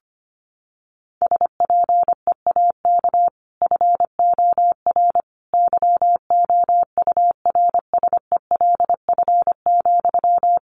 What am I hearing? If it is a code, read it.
Morse code, heard as a keyed tone, about 25 wpm, 700 Hz. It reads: SPEAK FOR YOURSELF,
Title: Sentences from Top 1100 Words 25wpm